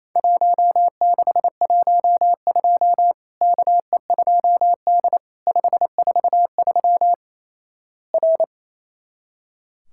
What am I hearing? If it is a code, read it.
Morse code, heard as a keyed tone, about 28 wpm, 700 Hz. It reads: 1612 KE2D 543 R
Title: Worked All Europe DX Contest - QTC Exchange - 28wpm